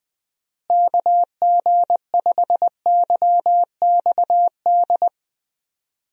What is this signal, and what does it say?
Morse code, heard as a keyed tone, about 20 wpm, 700 Hz. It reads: KG5YXD
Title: USA Call Signs - 6 Characters - Repeated 3x - 20wpm